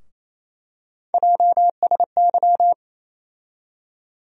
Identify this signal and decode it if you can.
Morse code, heard as a keyed tone, about 28 wpm, 700 Hz. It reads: JSY
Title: USA Call Sign Postfixes - 3 Letters 28wpm